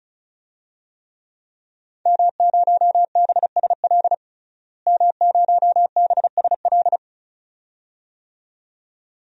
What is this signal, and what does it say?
Morse code, heard as a keyed tone, about 34 wpm, 700 Hz. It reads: M0BSL M0BSL
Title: DXCC-100 Call Signs - 5 Characters - Repeated 2x - 35wpm